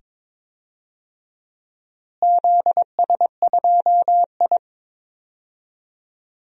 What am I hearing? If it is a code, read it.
Morse code, heard as a keyed tone, about 22 wpm, 700 Hz. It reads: ZS2I
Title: DXCC-100 Call Signs - 4 Characters - Repeated 2x - 22wpm